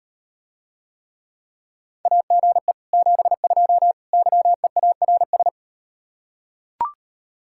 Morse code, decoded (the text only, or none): AGE 72 YEARS